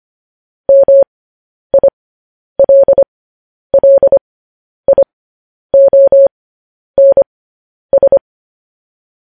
Morse code, heard as a keyed tone, about 25 wpm, spaced as for 10 wpm, 550 Hz. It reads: MILLIONS
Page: news-headlines-in-morse-code-at-10-wpm